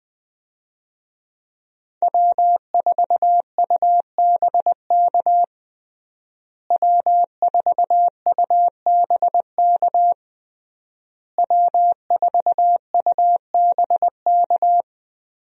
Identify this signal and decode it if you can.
Morse code, heard as a keyed tone, about 20 wpm, 700 Hz. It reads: W4UBK W4UBK W4UBK
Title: USA Call Signs - 5 Characters - Repeated 3x - 20wpm